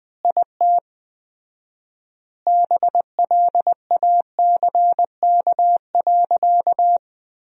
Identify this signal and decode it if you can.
Morse code, heard as a keyed tone, about 20 wpm, 700 Hz. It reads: IT BLACK.
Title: 4x Word-Spacing - Sentences from Top 400 Words - 20wpm